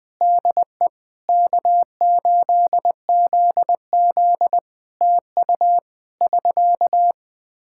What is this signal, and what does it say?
Morse code, heard as a keyed tone, about 20 wpm, 700 Hz. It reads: DE K8ZZ TU <SK>
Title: Practice QSO - Exchange 4 - Ending - 20wpm